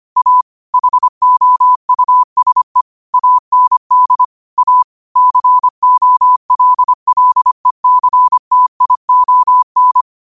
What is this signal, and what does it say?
Morse code, heard as a keyed tone, about 25 wpm, 1000 Hz. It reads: A HOUSE AND A COLLECTION
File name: qotdfiles / 25wpm